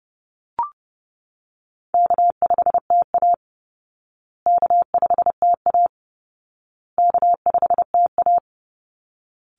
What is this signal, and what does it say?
Morse code, heard as a keyed tone, about 30 wpm, 700 Hz. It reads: K5TA K5TA K5TA
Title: Call Signs Repeated 2x - 30wpm